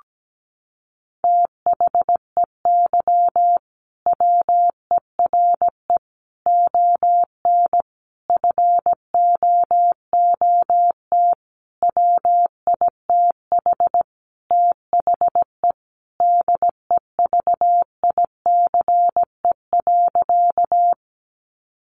Morse code, heard as a keyed tone, about 17 wpm, 700 Hz. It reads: THEY WERE ON FOOT WITH THE DEVICE.
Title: Sentences from Top 1500 Words 17wpm